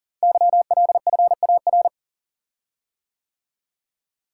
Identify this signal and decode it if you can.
Morse code, heard as a keyed tone, about 42 wpm, 690 Hz. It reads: YLFAR
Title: Code Groups - 5 Letters - 40wpm